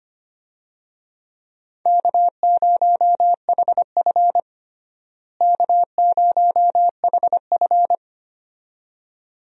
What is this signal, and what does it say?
Morse code, heard as a keyed tone, about 25 wpm, 700 Hz. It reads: K0HF K0HF
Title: USA Call Signs - 4 Characters - Repeated 2x - 25wpm